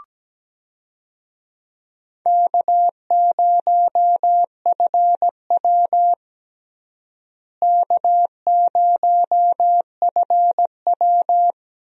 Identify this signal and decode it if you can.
Morse code, heard as a keyed tone, about 17 wpm, 700 Hz. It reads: K0FW K0FW
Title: USA Call Signs - 4 Characters - Repeated 3x - 17wpm